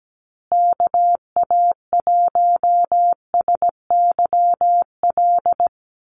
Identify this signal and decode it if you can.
Morse code, heard as a keyed tone, about 17 wpm, 700 Hz. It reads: KA1SYL
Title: USA Call Signs Repeated 3x 17wpm